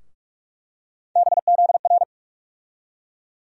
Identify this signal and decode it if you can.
Morse code, heard as a keyed tone, about 44 wpm, 700 Hz. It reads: BZR
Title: USA Call Sign Postfixes - 3 Letters 45wpm